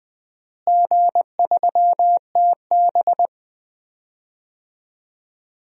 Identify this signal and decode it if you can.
Morse code, heard as a keyed tone, about 21 wpm, 700 Hz. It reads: G3TB
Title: DXCC-100 Call Signs - 4 Characters - Repeated 2x - 20wpm